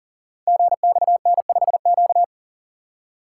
Morse code, heard as a keyed tone, about 39 wpm, 700 Hz. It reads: GXN5Q
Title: Code Groups - 5 Letter:Numbers - 40wpm